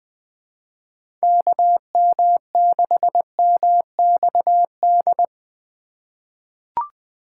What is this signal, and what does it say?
Morse code, heard as a keyed tone, about 20 wpm, 700 Hz. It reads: KM6MXD
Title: USA Call Signs - 6 Characters - 4x Character Spacing - 20wpm